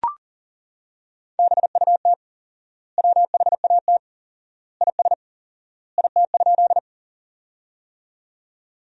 Morse code, heard as a keyed tone, about 40 wpm, 700 Hz. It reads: BUT WHAT IS IT?
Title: 4x Word-Spacing - Sentences from Top 100 Words - 40wpm